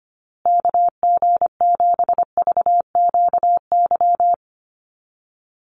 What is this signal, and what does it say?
Morse code, heard as a keyed tone, about 25 wpm, 700 Hz. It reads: KG7VQY
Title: USA Call Signs Repeated 3x 25wpm